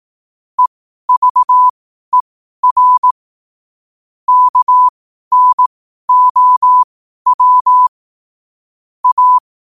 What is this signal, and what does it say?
Morse code, heard as a keyed tone, about 18 wpm, spaced as for 12 wpm, 1000 Hz. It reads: EVER KNOW A
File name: qotdfiles / 12wpm